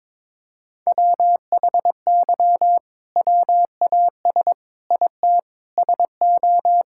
Morse code, heard as a keyed tone, about 22 wpm, 700 Hz. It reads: WHY WAS IT SO
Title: Sentences from Top 1000 Words - Review Part 5 - 22wpm